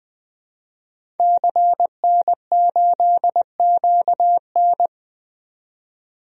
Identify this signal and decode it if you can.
Morse code, heard as a keyed tone, about 20 wpm, 700 Hz. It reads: CN8QN
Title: DXCC-100 Call Signs - 5 Characters - 20wpm